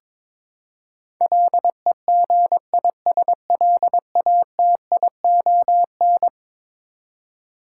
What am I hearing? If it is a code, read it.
Morse code, heard as a keyed tone, about 22 wpm, 700 Hz. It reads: LEGISLATION
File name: practice / 22wpm / Eleven Letter Words - 22wpm